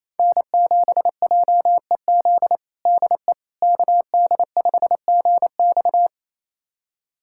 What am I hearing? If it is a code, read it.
Morse code, heard as a keyed tone, about 28 wpm, 700 Hz. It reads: N7JEZ DE KD5GX
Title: USA Callsign DE Callsign - 5 Characters - 28wpm